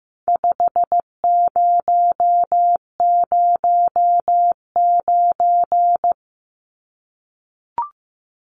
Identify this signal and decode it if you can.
Morse code, heard as a keyed tone, about 15 wpm, 700 Hz. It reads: 5009
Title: Numbers - 4-Digit 15wpm